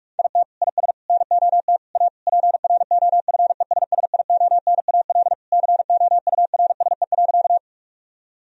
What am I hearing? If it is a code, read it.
Morse code, heard as a keyed tone, about 41 wpm, 700 Hz. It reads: IT IS NOT A PROFESSIONAL COURSE.
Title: Sentences from Top 1000 Words - Review Part 4 - 45wpm